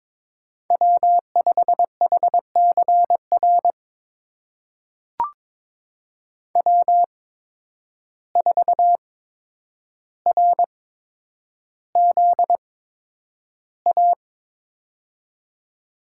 Morse code, heard as a keyed tone, about 22 wpm, spaced as for 9 wpm, 700 Hz. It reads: W5HCR W4RZA
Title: USA Call Signs - 5 Characters - 8x Character Spacing - 22wpm